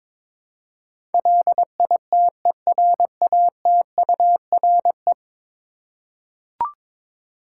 Morse code, heard as a keyed tone, about 22 wpm, 700 Hz. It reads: LITERATURE
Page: morse-code-ninja-practice-22wpm